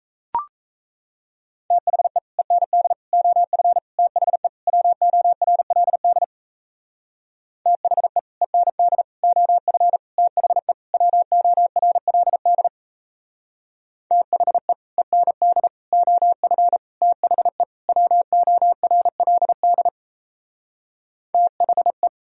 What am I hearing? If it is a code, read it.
Morse code, heard as a keyed tone, about 37 wpm, 700 Hz. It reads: THE END OF THE WORLD THE END OF THE WORLD THE END OF THE WORLD THE